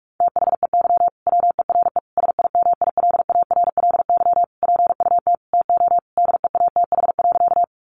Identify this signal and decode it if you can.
Morse code, heard as a keyed tone, about 44 wpm, 700 Hz. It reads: THEY WERE SIMILARLY PUT TO DEATH.
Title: Sentences from Top 1400 Words 45wpm